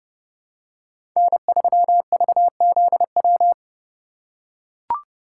Morse code, heard as a keyed tone, about 31 wpm, 690 Hz. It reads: N3VZW
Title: USA Call Signs - 5 Characters - 2x Character Spacing - 30wpm